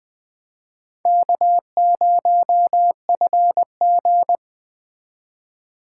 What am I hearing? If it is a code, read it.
Morse code, heard as a keyed tone, about 20 wpm, 700 Hz. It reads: K0FG
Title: USA Call Signs - 4 Characters - 4x Character Spacing - 20wpm